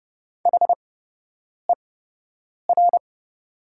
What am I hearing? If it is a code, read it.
Morse code, heard as a keyed tone, about 31 wpm, spaced as for 8 wpm, 700 Hz. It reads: HER
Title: 8x Character Spacing - Top 100 Words - 30wpm